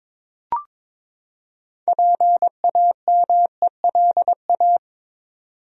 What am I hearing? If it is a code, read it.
Morse code, heard as a keyed tone, about 21 wpm, 700 Hz. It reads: PAMELA
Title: USA Common Names - 22wpm